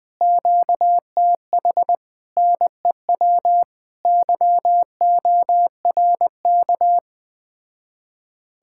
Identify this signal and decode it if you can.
Morse code, heard as a keyed tone, about 20 wpm, 700 Hz. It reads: QTH NEW YORK
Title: Speed-Racing Common QSO Phrases Encore - 17wpm